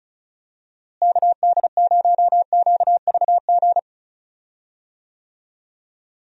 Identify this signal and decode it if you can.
Morse code, heard as a keyed tone, about 34 wpm, 700 Hz. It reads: KD0QVG
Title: USA Call Signs - 6 Characters - Repeated 3x - 35wpm